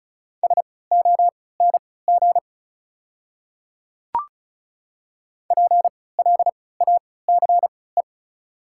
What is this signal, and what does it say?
Morse code, heard as a keyed tone, about 36 wpm, spaced as for 20 wpm, 690 Hz. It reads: SONG PLACE
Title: 3x Character Spacing - Top 500 Words - 35wpm